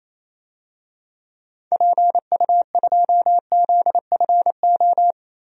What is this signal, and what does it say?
Morse code, heard as a keyed tone, about 28 wpm, 700 Hz. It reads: PU2ZFO
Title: DXCC-100 Call Signs - 6 Characters - Repeated 2x - 28wpm